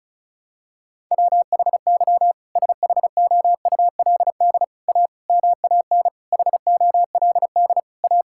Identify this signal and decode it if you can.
Morse code, heard as a keyed tone, about 34 wpm, 700 Hz. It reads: WHY SHOULD A MAN HOLD A
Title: Sentences from Top 1600 Words 35wpm